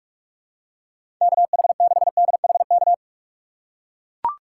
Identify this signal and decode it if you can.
Morse code, heard as a keyed tone, about 41 wpm, 700 Hz. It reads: KH6DHK
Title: K1USN SST QSO Practice - 45wpm